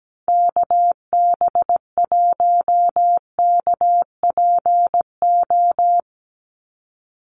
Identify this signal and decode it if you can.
Morse code, heard as a keyed tone, about 17 wpm, 700 Hz. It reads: KB1KPO
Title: USA Call Signs 17wpm